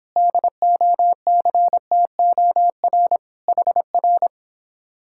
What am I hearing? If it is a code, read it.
Morse code, heard as a keyed tone, about 26 wpm, 700 Hz. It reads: DOCTOR HR